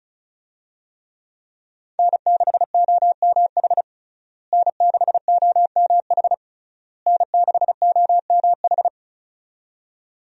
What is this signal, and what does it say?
Morse code, heard as a keyed tone, about 34 wpm, 700 Hz. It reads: N6OMH N6OMH N6OMH
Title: USA Call Signs - 5 Characters - Repeated 3x - 35wpm